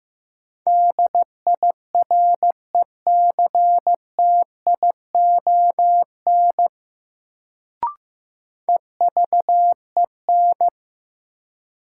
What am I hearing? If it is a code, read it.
Morse code, heard as a keyed tone, about 15 wpm, 700 Hz. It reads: DIRECTION EVEN
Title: Sets of 4 Words - New Word Drill - 15wpm